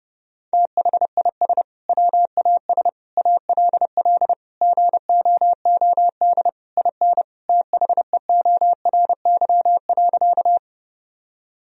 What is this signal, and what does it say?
Morse code, heard as a keyed tone, about 32 wpm, 700 Hz. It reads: THIS WAS ALL GOOD IN THEORY.
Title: Sentences from Top 1000 Words - Review Part 3 - 30wpm